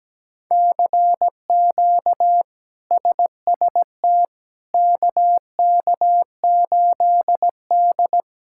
Morse code, heard as a keyed tone, about 17 wpm, 700 Hz. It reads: CQ SST KK8D
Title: K1USN SST QSO Practice - 17wpm